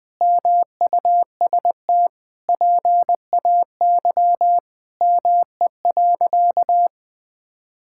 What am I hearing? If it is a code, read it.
Morse code, heard as a keyed tone, about 20 wpm, 700 Hz. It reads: MUST PAY ME.
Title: Sentences from Top 1000 Words - Review Part 5 - 20wpm